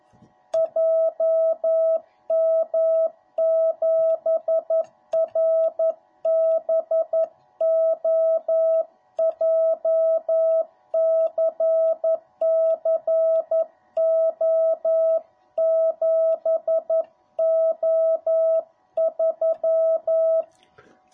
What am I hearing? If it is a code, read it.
Morse code, heard as a keyed tone, about 11 wpm, 660 Hz. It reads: JM7RBO JCCO7O3